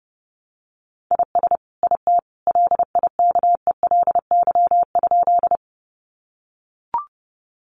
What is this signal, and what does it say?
Morse code, heard as a keyed tone, about 30 wpm, 700 Hz. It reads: IS IT LIKELY?
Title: Sentences from Top 600 Words 30wpm